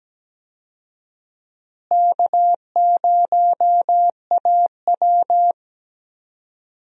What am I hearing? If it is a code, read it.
Morse code, heard as a keyed tone, about 17 wpm, 700 Hz. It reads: K0AW
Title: USA Call Signs - 4 Characters - Repeated 3x - 17wpm